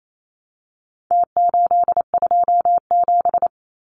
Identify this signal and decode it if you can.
Morse code, heard as a keyed tone, about 28 wpm, 700 Hz. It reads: T827
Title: Numbers - Contest Style - 4-Digit 28wpm